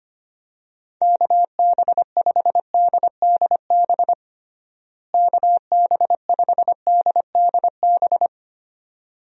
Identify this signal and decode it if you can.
Morse code, heard as a keyed tone, about 25 wpm, 700 Hz. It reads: KB5DDB KB5DDB
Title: USA Call Signs - 6 Characters - Repeated 2x - 25wpm